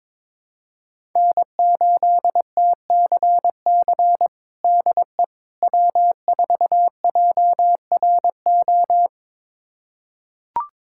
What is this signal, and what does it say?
Morse code, heard as a keyed tone, about 22 wpm, 700 Hz. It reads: N8TCC DE W4JRO
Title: USA Callsign DE Callsign - 5 Characters - 22wpm